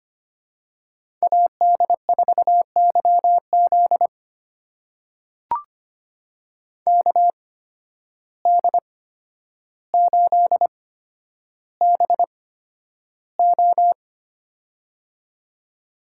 Morse code, heard as a keyed tone, about 25 wpm, spaced as for 11 wpm, 700 Hz. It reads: AD4YZ KD8BO
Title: USA Call Signs - 5 Characters - 8x Character Spacing - 25wpm